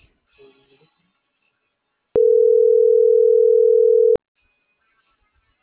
Ringback tone (North America), 440 + 480 Hz.